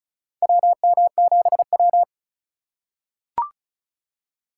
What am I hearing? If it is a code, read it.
Morse code, heard as a keyed tone, about 36 wpm, 700 Hz. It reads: WM7W
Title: USA Call Signs - 4 Characters - 3x Character Spacing - 35wpm